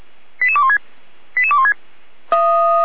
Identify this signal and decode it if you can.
Five-tone selective call (ZVEI), heard as quick runs of five tones, tones of about 70 ms. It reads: 88216 88216